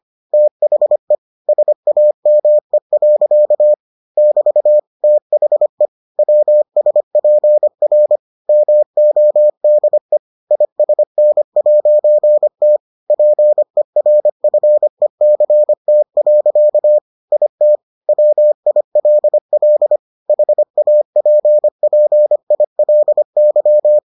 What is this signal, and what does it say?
Morse code, heard as a keyed tone, about 25 wpm, 600 Hz. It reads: THE SAME. <BT> THE WSPR MODE ISN'T PERFECT. IT WILL HAPPILY